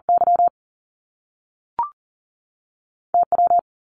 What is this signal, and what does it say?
Morse code, heard as a keyed tone, about 42 wpm, 700 Hz. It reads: Y T W
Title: Top Two Letter Combinations in Words 40wpm